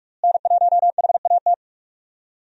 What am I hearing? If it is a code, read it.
Morse code, heard as a keyed tone, about 48 wpm, 700 Hz. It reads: N1HAT
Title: USA Call Signs - 5 Characters - Repeated 2x - 45wpm